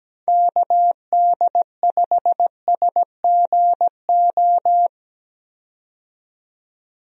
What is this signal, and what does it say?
Morse code, heard as a keyed tone, about 17 wpm, 700 Hz. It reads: KD5SGO
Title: USA Call Signs - 6 Characters - 17wpm